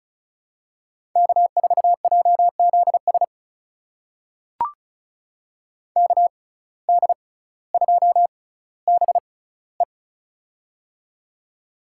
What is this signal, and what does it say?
Morse code, heard as a keyed tone, about 36 wpm, spaced as for 18 wpm, 700 Hz. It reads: K4JZS KD2BE
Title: USA Call Signs - 5 Characters - 6x Character Spacing - 35wpm